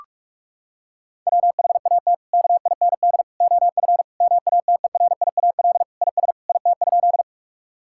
Morse code, heard as a keyed tone, about 46 wpm, 700 Hz. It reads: WHAT KIND OF MATERIAL IS IT?
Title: Sentences from Top 1000 Words - Review Part 5 - 45wpm